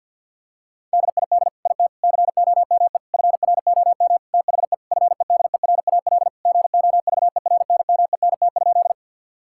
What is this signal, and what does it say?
Morse code, heard as a keyed tone, about 52 wpm, 700 Hz. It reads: DID IT COME FROM THE FEDERAL GOVERNMENT?
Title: Sentences from Top 1000 Words - Review Part 5 - 50wpm